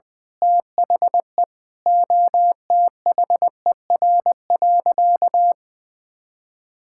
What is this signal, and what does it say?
Morse code, heard as a keyed tone, about 20 wpm, 700 Hz. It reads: THE OTHER.
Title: Sentences from Top 1000 Words - Review Part 6 - 20wpm